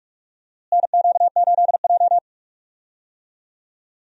Morse code, heard as a keyed tone, about 47 wpm, 700 Hz. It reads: NQ8J
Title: USA Call Signs - 4 Characters - Repeated 2x - 45wpm